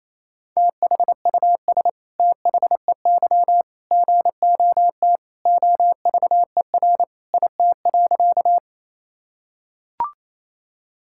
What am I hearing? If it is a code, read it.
Morse code, heard as a keyed tone, about 28 wpm, 700 Hz. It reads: THUS THEY GOT OVER IT.
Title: Sentences from Top 1000 Words - Review Part 5 - 28wpm